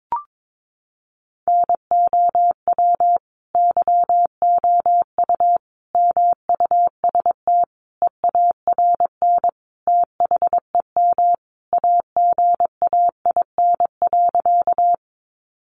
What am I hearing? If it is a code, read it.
Morse code, heard as a keyed tone, about 22 wpm, 700 Hz. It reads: NOW YOU MUST EARN THEM AGAIN.
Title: Sentences from Top 1500 Words 22wpm